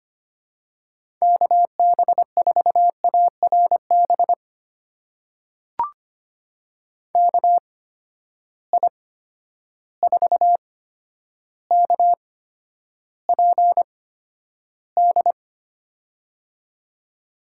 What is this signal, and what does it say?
Morse code, heard as a keyed tone, about 25 wpm, spaced as for 11 wpm, 700 Hz. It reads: KB4ARB KI4KPD
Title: USA Call Signs - 8x Character Spacing - 25wpm